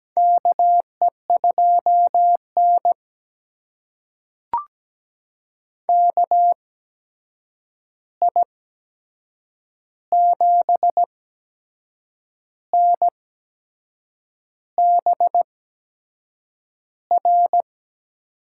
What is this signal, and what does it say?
Morse code, heard as a keyed tone, about 17 wpm, spaced as for 6 wpm, 700 Hz. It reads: KE2N KI7NBR
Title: USA Call Signs - 8x Character Spacing - 17wpm